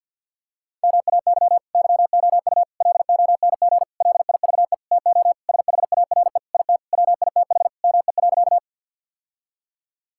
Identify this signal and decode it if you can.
Morse code, heard as a keyed tone, about 46 wpm, 700 Hz. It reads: MAY YOU LONG LIVE TO SHARE IT WITH ME.
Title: Sentences from Top 1000 Words - Review Part 6 - 50wpm